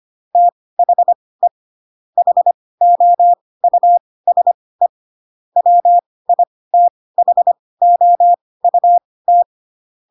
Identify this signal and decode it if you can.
Morse code, heard as a keyed tone, about 25 wpm, spaced as for 18 wpm, 700 Hz. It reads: THE HOUSE WITHOUT